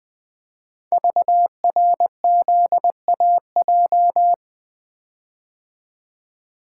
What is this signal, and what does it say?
Morse code, heard as a keyed tone, about 20 wpm, 700 Hz. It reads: VRZAJ
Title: Code Groups - 5 Letters - 20wpm